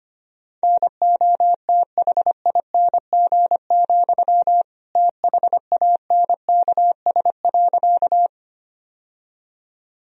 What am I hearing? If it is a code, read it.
Morse code, heard as a keyed tone, about 25 wpm, 700 Hz. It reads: NOTHING, THANKS.
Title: Sentences from Top 1000 Words - Review Part 3 - 25wpm